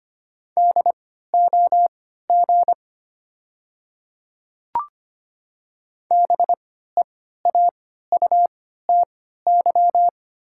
Morse code, heard as a keyed tone, about 25 wpm, spaced as for 14 wpm, 700 Hz. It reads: DOG BEAUTY
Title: 3x Character Spacing - Top 500 Words - 25wpm